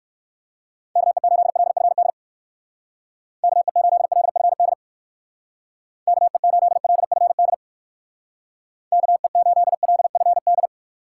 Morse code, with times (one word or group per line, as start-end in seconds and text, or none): 1.0-2.1 KE8LFD
3.4-4.7 KE8LFD
6.1-7.5 KE8LFD
8.9-10.7 KE8LFD